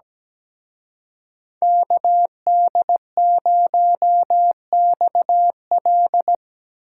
Morse code, heard as a keyed tone, about 19 wpm, 700 Hz. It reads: KD0XL
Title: Speed Racing - USA 2x2 Call Signs 17wpm